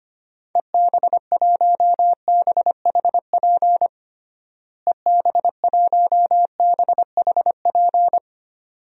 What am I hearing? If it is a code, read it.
Morse code, heard as a keyed tone, about 25 wpm, 700 Hz. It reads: EB1BHP EB1BHP
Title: DXCC-100 Call Signs - 6 Characters - Repeated 2x - 25wpm